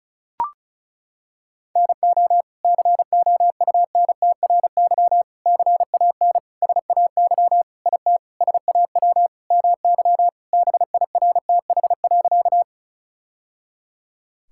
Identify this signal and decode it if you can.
Morse code, heard as a keyed tone, about 36 wpm, 700 Hz. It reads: NO COUNTRY CAN SAY IT SAW MY BIRTH.
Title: Sentences from Top 1000 Words - Review Part 5 - 35wpm